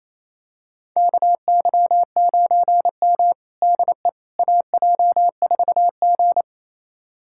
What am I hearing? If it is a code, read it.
Morse code, heard as a keyed tone, about 27 wpm, 690 Hz. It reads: KY9M DE AJ4G
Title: USA Callsign DE Callsign - 4 Characters - 28wpm